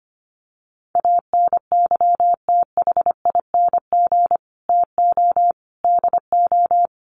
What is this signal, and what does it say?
Morse code, heard as a keyed tone, about 25 wpm, 700 Hz. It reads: ANYTHING TO DO
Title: Sets of 5 Words Encore 25wpm